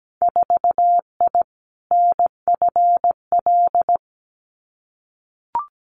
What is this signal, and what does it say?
Morse code, heard as a keyed tone, about 17 wpm, 700 Hz. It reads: 4I NFL
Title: Winter Field Day Exchanges 17wpm